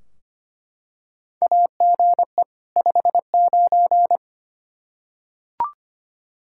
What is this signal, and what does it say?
Morse code, heard as a keyed tone, about 25 wpm, 700 Hz. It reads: AGE 59
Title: Practice QSO - Exchange 3 - More Info - 25wpm